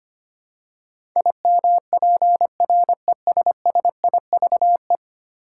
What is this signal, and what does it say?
Morse code, heard as a keyed tone, about 25 wpm, 700 Hz. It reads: IMPRESSIVE
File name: 10 Letter Words - First 3 Letters Given - 25wpm